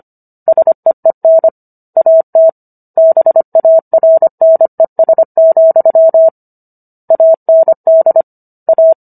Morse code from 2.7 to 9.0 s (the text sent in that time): BARNES, AND A